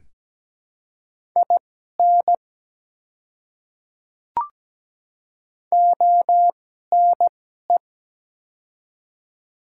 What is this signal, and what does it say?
Morse code, heard as a keyed tone, about 17 wpm, spaced as for 12 wpm, 700 Hz. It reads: IN ONE
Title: 2x Character Spacing - Top 200 Words - 17wpm